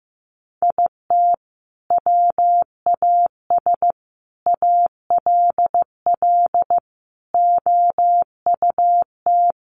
Morse code, heard as a keyed tone, about 15 wpm, 700 Hz. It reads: IT WAS ALL OUT
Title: Sentences from Top 1200 Words 15wpm